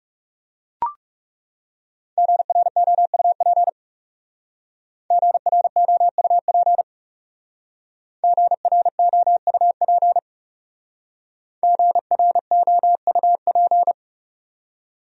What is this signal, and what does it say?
Morse code, heard as a keyed tone, about 38 wpm, 700 Hz. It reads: GROUP GROUP GROUP GROUP